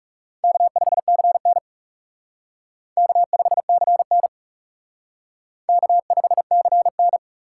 Morse code, heard as a keyed tone, about 37 wpm, 700 Hz. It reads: K5CN K5CN K5CN
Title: Speed Racing - USA 1x2 Call Signs 30wpm